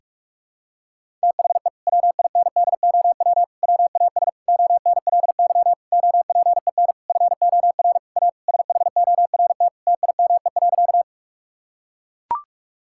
Morse code, heard as a keyed tone, about 44 wpm, 700 Hz. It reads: THE WINDOW WAS ONLY OPEN FOR A SHORT TIME.
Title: Sentences from Top 1600 Words 45wpm